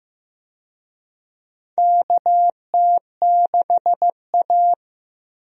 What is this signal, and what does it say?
Morse code, heard as a keyed tone, about 15 wpm, 700 Hz. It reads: KT6A
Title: USA Call Signs - 4 Characters - Repeated 2x - 15wpm